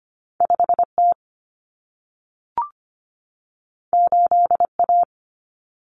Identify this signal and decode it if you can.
Morse code, heard as a keyed tone, about 25 wpm, 700 Hz. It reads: 5T 8A
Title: Common Cut Numbers - 2-Digit 25wpm